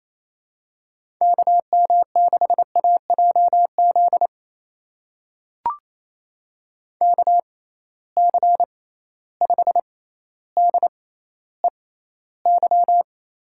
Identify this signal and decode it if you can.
Morse code, heard as a keyed tone, about 28 wpm, spaced as for 14 wpm, 700 Hz. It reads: KM6AJZ KC5DEY
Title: USA Call Signs - 6 Characters - 6x Character Spacing - 28wpm